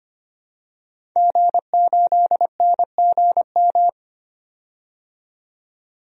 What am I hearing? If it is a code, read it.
Morse code, heard as a keyed tone, about 25 wpm, 690 Hz. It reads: G8NGM
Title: DXCC-100 Call Signs - 5 Characters - 25wpm